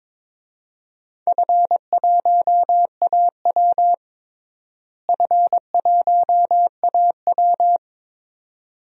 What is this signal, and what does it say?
Morse code, heard as a keyed tone, about 22 wpm, 700 Hz. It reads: F1AW F1AW
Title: DXCC-100 Call Signs - 4 Characters - Repeated 2x - 22wpm